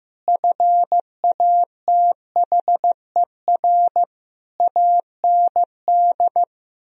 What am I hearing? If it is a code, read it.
Morse code, heard as a keyed tone, about 15 wpm, 700 Hz. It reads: FATHER AND
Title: Sentences from Top 1000 Words - Review Part 3 - 15wpm